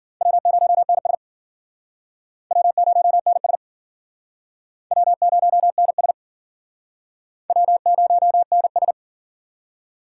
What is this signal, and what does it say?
Morse code, heard as a keyed tone, about 50 wpm, 700 Hz. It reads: W0NS W0NS W0NS W0NS
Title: Speed Racing - USA 1x2 Call Signs 40wpm